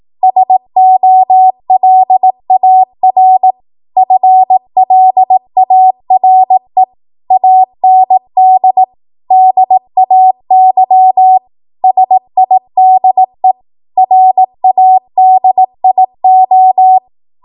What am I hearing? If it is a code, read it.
Morse code, heard as a keyed tone, about 18 wpm, 750 Hz. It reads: SOLAR FLARE AND DAY SIDE RADIO
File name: Archive / 18 WPM / 18WPM